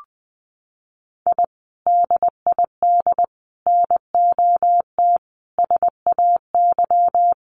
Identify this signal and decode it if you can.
Morse code, heard as a keyed tone, about 20 wpm, 700 Hz. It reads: I DID NOT SAY
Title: Sets of 5 Words Encore2 20wpm